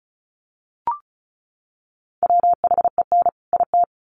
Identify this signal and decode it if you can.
Morse code, heard as a keyed tone, about 34 wpm, 700 Hz. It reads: WHEN IT
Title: Sets of 2 Words 35wpm